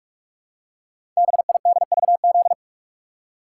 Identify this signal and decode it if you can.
Morse code, heard as a keyed tone, about 44 wpm, 700 Hz. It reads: BIDVZ
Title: Code Groups - 5 Letter:Numbers - 45wpm